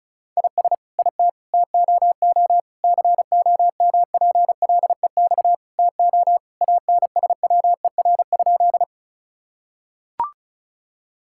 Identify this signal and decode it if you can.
Morse code, heard as a keyed tone, about 36 wpm, 700 Hz. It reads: IS IT TOO COMPLEX TO ANSWER?
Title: Sentences from Top 1000 Words - Review Part 6 - 35wpm